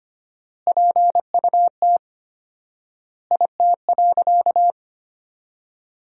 Morse code, heard as a keyed tone, about 25 wpm, 700 Hz. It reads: PUT IT.
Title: 4x Word-Spacing - Sentences from Top 200 Words - 25wpm